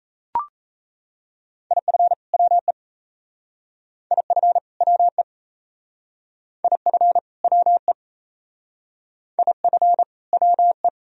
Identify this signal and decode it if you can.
Morse code, heard as a keyed tone, about 37 wpm, 700 Hz. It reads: IF WE IF WE IF WE IF WE